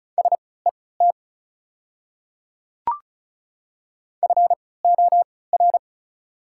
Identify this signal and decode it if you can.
Morse code, heard as a keyed tone, about 34 wpm, spaced as for 20 wpm, 690 Hz. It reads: SET FOR
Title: 3x Character Spacing - Top 200 Words - 35wpm